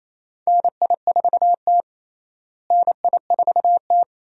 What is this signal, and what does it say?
Morse code, heard as a keyed tone, about 28 wpm, 700 Hz. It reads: NI4T NI4T
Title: USA Call Signs - 4 Characters - Repeated 2x - 28wpm